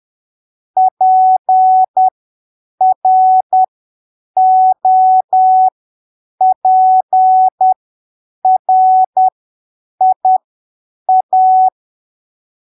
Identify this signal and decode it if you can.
Morse code, heard as a keyed tone, about 10 wpm, spaced as for 7 wpm, 750 Hz. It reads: PROPRIA
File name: arquivos / texto / texto 04 10ppm